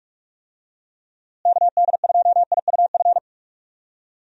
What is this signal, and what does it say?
Morse code, heard as a keyed tone, about 42 wpm, 700 Hz. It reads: KD2IUF
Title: USA Call Signs - 6 Characters - Repeated 2x - 45wpm